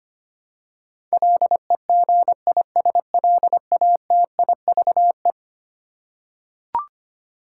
Morse code, heard as a keyed tone, about 25 wpm, 700 Hz. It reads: LEGISLATIVE
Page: morse-code-ninja-practice-25wpm